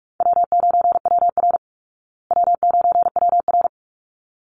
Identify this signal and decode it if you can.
Morse code, heard as a keyed tone, about 44 wpm, 700 Hz. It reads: W9WR W9WR
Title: Easy Call Signs Repeated 2x - 45wpm